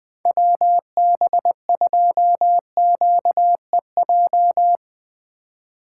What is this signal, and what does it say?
Morse code, heard as a keyed tone, about 20 wpm, 700 Hz. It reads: WB2QEJ
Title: Speed Racing - USA 2x3 Call Signs 17wpm